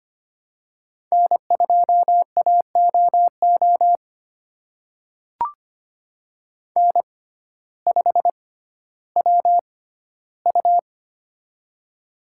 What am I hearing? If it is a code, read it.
Morse code, heard as a keyed tone, about 25 wpm, spaced as for 14 wpm, 700 Hz. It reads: N2AOO N5WU
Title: USA Call Signs - 6x Character Spacing - 25wpm